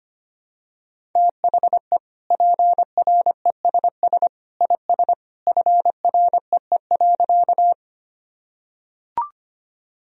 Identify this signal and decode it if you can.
Morse code, heard as a keyed tone, about 25 wpm, 700 Hz. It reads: THE PRESS IS FREE.
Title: Sentences from Top 1000 Words - Review Part 1 - 25wpm